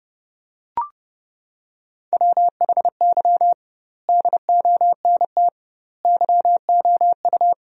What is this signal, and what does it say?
Morse code, heard as a keyed tone, about 30 wpm, 700 Hz. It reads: WHY DONT YOU
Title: 2x Word-Spacing - Sentences from Top 300 Words - 30wpm